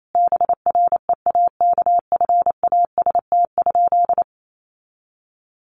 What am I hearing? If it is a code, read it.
Morse code, heard as a keyed tone, about 28 wpm, 700 Hz. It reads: BREAKFAST?
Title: Sentences from Top 1500 Words 28wpm